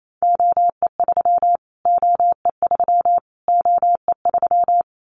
Morse code, heard as a keyed tone, about 28 wpm, 700 Hz. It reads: OE3 OE3 OE3
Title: Callsign Prefixes for Top 25 DXCC - Repeated 3x - 28wpm